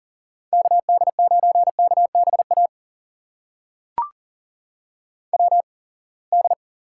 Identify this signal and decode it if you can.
Morse code, heard as a keyed tone, about 40 wpm, spaced as for 28 wpm, 700 Hz. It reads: KD9KBA WD
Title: USA Call Signs - 6 Characters - 8x Character Spacing - 40wpm